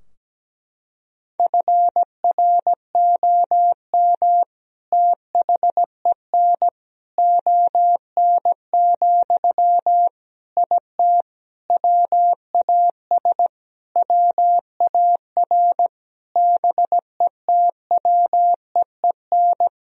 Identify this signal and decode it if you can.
Morse code, heard as a keyed tone, about 17 wpm, 700 Hz. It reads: FROM THEN ON, IT WAS WAR BETWEEN
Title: Sentences from Top 1000 Words - Review Part 2 - 17wpm